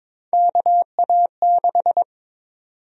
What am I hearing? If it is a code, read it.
Morse code, heard as a keyed tone, about 22 wpm, 700 Hz. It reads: KA6
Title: USA Call Sign Prefixes - Repeated 2x - 22wpm